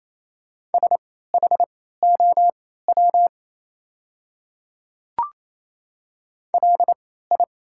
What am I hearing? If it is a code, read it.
Morse code, heard as a keyed tone, about 27 wpm, spaced as for 16 wpm, 690 Hz. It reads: SHOW LI
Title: 3x Character Spacing - Top 300 Words - 28wpm